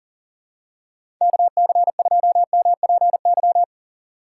Morse code, heard as a keyed tone, about 39 wpm, 690 Hz. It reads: KC2MPY
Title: USA Call Signs - 2x Character Spacing - 40wpm